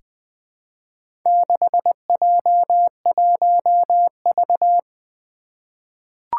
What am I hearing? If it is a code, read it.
Morse code, heard as a keyed tone, about 20 wpm, 700 Hz. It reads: 6J1V
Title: Four Challenging Letter-Number Combinations 20wpm